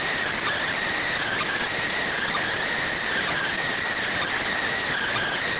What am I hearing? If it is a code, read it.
Emergency Alert System (SAME) tones, heard as harsh buzzing bursts. It is heard from 1.2 to 5.4 s.